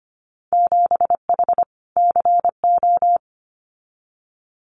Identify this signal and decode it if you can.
Morse code, heard as a keyed tone, about 25 wpm, 700 Hz. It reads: 7H CO
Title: Winter Field Day Exchanges Encore 25wpm